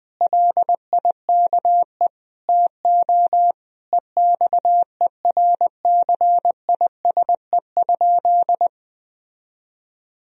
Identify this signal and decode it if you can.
Morse code, heard as a keyed tone, about 21 wpm, 700 Hz. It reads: LIKE TO EXERCISE?
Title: Sentences from Top 1000 Words - Review Part 4 - 20wpm